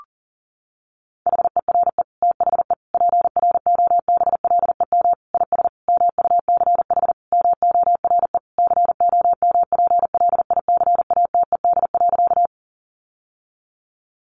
Morse code, heard as a keyed tone, about 40 wpm, 700 Hz. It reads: HERE THE PROBLEM IS MUCH MORE COMPLICATED.
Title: Sentences from Top 1200 Words 40wpm